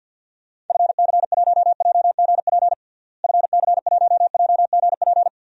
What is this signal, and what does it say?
Morse code, heard as a keyed tone, about 48 wpm, 700 Hz. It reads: FC1JGP FC1JGP
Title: DXCC-100 Call Signs - 6 Characters - Repeated 3x - 50wpm